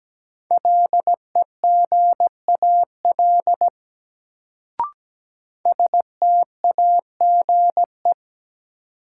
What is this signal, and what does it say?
Morse code, heard as a keyed tone, about 17 wpm, 700 Hz. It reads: LEGAL STAGE
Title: Sentences from Top 1200 Words - New Word Drill Alt - 17wpm